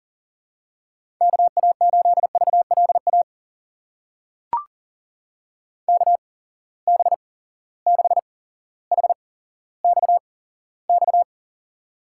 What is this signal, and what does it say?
Morse code, heard as a keyed tone, about 40 wpm, spaced as for 17 wpm, 700 Hz. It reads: KA8VLA KB6HXX
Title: USA Call Signs - 6 Characters - 8x Character Spacing - 40wpm